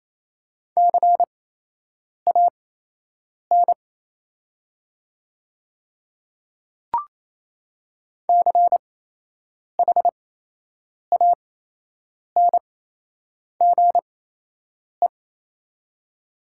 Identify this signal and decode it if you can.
Morse code, heard as a keyed tone, about 28 wpm, spaced as for 8 wpm, 700 Hz. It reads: CAN CHANGE
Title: 8x Character Spacing - Top 200 Words - 28wpm